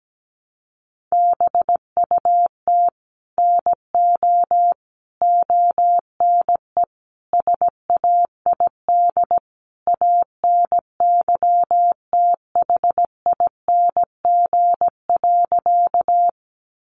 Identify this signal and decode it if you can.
Morse code, heard as a keyed tone, about 17 wpm, 700 Hz. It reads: BUT NO ONE SAID ANYTHING.
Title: Sentences from Top 1000 Words 17wpm